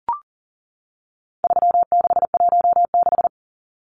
Morse code, heard as a keyed tone, about 40 wpm, 700 Hz. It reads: 3616
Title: Numbers - 4-Digit 40wpm